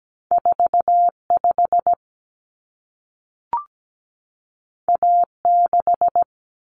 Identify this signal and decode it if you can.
Morse code, heard as a keyed tone, about 17 wpm, 700 Hz. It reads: 45 A6
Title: Common Cut Numbers - 2-Digit 17wpm